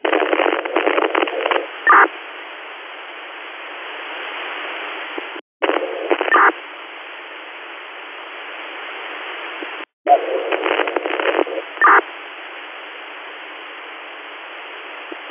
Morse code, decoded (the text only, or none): none